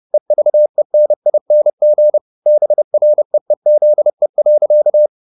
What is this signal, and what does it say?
Morse code, heard as a keyed tone, about 30 wpm, 600 Hz. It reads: EVENING BREEZE.
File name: the old man and the sea / the old man and the sea 1 30wpm